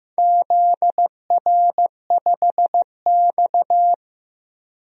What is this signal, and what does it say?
Morse code, heard as a keyed tone, about 15 wpm, 700 Hz. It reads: ZR5X
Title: DXCC-100 Call Signs - 4 Characters - Repeated 3x - 15wpm